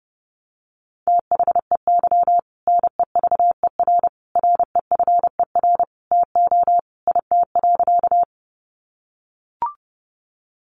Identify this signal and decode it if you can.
Morse code, heard as a keyed tone, about 30 wpm, 700 Hz. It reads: THEY NEVER REFER TO IT.
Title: Sentences from Top 1000 Words 30wpm